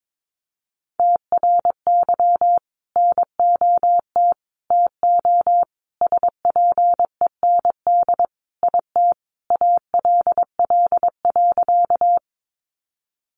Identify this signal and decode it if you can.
Morse code, heard as a keyed tone, about 22 wpm, 700 Hz. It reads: TRY NOT TO SPEND IT ALL.
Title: Sentences from Top 800 Words 22wpm